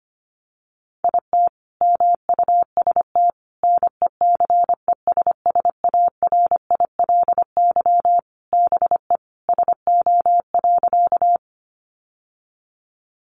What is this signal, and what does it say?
Morse code, heard as a keyed tone, about 25 wpm, 700 Hz. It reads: IT MUST NECESSARILY BE SO.
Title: Sentences from Top 1400 Words 25wpm